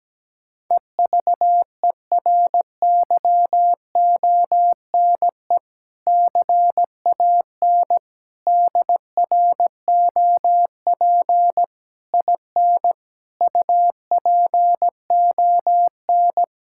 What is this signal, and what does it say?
Morse code, heard as a keyed tone, about 17 wpm, 700 Hz. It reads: EVERYONE CAN DROP IN UPON
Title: Sentences from Top 1600 Words 17wpm